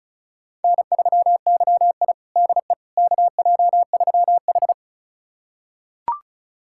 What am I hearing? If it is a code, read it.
Morse code, heard as a keyed tone, about 36 wpm, 700 Hz. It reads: N3YI DE KJ3H
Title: USA Callsign DE Callsign - 4 Characters - 35wpm